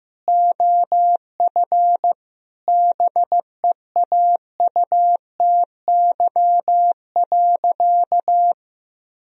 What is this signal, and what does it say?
Morse code, heard as a keyed tone, about 15 wpm, 700 Hz. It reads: OF BEAUTY.
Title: Sentences from Top 1000 Words - Review Part 2 - 15wpm